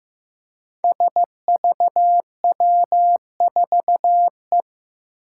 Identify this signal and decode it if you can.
Morse code, heard as a keyed tone, about 15 wpm, 700 Hz. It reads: SVW4E
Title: Five Challenging Letter-Number Combinations 15wpm